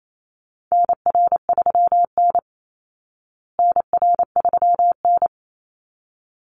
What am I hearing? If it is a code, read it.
Morse code, heard as a keyed tone, about 28 wpm, 700 Hz. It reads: NR3N NR3N
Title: Call Signs Repeated 3x - 28wpm